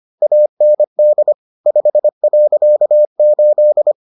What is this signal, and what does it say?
Morse code, heard as a keyed tone, about 25 wpm, 600 Hz. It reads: AND 5.8